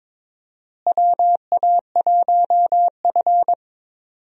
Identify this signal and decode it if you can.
Morse code, heard as a keyed tone, about 22 wpm, 700 Hz. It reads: WA1F
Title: USA Call Signs - 4 Characters - Repeated 2x - 22wpm